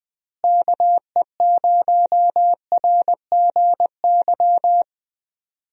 Morse code, heard as a keyed tone, about 21 wpm, 700 Hz. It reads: KE0RGY
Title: USA Call Signs - 6 Characters - Repeated 2x - 20wpm